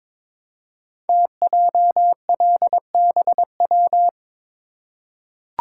Morse code, heard as a keyed tone, about 22 wpm, 700 Hz. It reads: TJLBW
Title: Code Groups - 5 Letter:Numbers - 22wpm